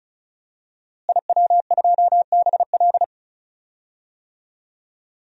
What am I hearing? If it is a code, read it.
Morse code, heard as a keyed tone, about 36 wpm, 700 Hz. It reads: IW2BL
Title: DXCC-100 Call Signs - 5 Characters - Repeated 3x - 35wpm